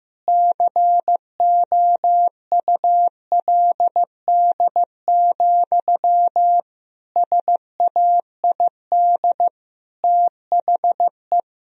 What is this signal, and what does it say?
Morse code, heard as a keyed tone, about 16 wpm, 700 Hz. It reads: COULD, SAID THE
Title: Sentences from Top 1000 Words - Review Part 4 - 15wpm